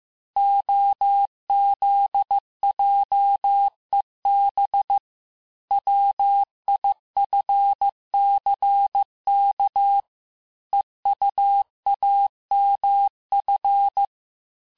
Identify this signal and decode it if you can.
Morse code, heard as a keyed tone, about 15 wpm, 800 Hz. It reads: OZJEB WIFCK EUAMF
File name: cours-du-lundi files / cours-1585-texte-chiffre-vitesse-14mpm